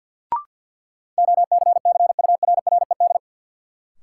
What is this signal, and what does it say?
Morse code, heard as a keyed tone, about 47 wpm, 700 Hz. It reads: OCCURRED
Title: Sets of 5 Words Encore - New Word Drill - 50wpm